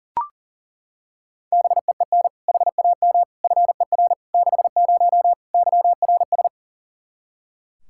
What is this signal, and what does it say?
Morse code, heard as a keyed tone, about 40 wpm, 700 Hz. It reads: BEEN HAM FER 60 YRS
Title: K1USN SST QSO Practice 40wpm